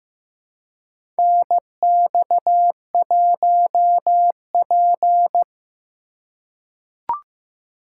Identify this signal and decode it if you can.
Morse code, heard as a keyed tone, about 15 wpm, 700 Hz. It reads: NX1P
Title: USA Call Signs - 4 Characters - 4x Character Spacing - 15wpm